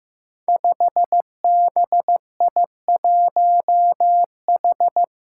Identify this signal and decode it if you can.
Morse code, heard as a keyed tone, about 15 wpm, 700 Hz. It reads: 5BI1H
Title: Five Challenging Letter-Number Combinations 15wpm